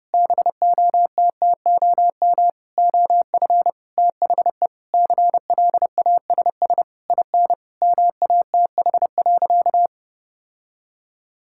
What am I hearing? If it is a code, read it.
Morse code, heard as a keyed tone, about 30 wpm, 700 Hz. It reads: BOTTOM OF THE CLASS IN MATH.
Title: Sentences from Top 1000 Words - Review Part 6 - 30wpm